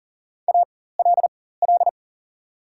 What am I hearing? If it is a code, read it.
Morse code, heard as a keyed tone, about 40 wpm, spaced as for 19 wpm, 700 Hz. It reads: ALL
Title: 4x Character Spacing - Top 200 Words - 40wpm